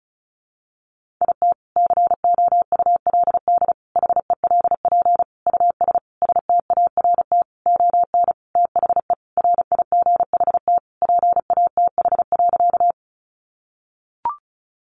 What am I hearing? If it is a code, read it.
Morse code, heard as a keyed tone, about 34 wpm, 700 Hz. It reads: IT COULD HELP US START ON THE RIGHT PATH.
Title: Sentences from Top 1400 Words 35wpm